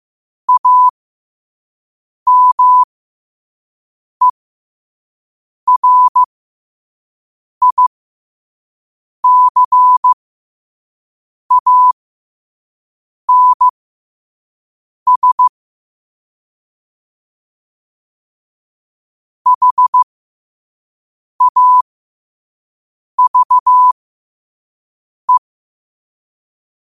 Morse code, heard as a keyed tone, about 15 wpm, spaced as for 5 wpm, 1000 Hz. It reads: AMERICANS HAVE
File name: qotdfiles / 5wpm